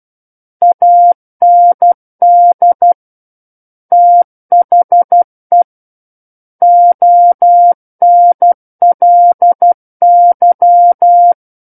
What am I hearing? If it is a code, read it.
Morse code, heard as a keyed tone, about 12 wpm, 700 Hz. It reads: AND THE ONLY
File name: Book One / 03-Chapter 3 - 12 WPM